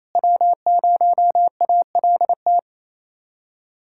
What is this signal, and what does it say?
Morse code, heard as a keyed tone, about 28 wpm, 700 Hz. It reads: W0ALT
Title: USA Callsign DE Callsign - 5 Characters - 28wpm